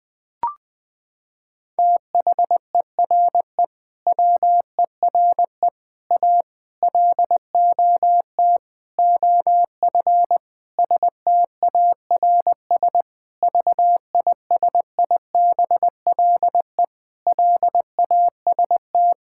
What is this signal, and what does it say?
Morse code, heard as a keyed tone, about 20 wpm, 700 Hz. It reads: THERE WERE A LOT OF STARS VISIBLE LAST
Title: Sentences with 5-Word Phrases - 20wpm